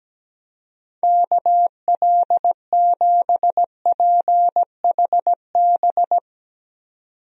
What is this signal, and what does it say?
Morse code, heard as a keyed tone, about 17 wpm, 700 Hz. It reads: KL7PHB
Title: DXCC-100 Call Signs - 6 Characters - 17wpm